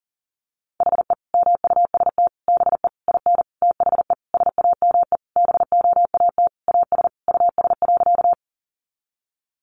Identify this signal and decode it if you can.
Morse code, heard as a keyed tone, about 40 wpm, 700 Hz. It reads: HE MUST BE IN THE SAME BOAT AS US.
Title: Sentences from Top 500 Words 40wpm